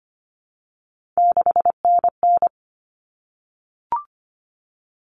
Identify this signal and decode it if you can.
Morse code, heard as a keyed tone, about 25 wpm, 700 Hz. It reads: 6NN
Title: Common Cut Numbers - 3-Digit 25wpm